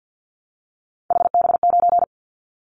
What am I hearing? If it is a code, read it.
Morse code, heard as a keyed tone, about 50 wpm, 700 Hz. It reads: HB9
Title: Callsign Prefixes for Top 25 DXCC - Repeated 3x - 50wpm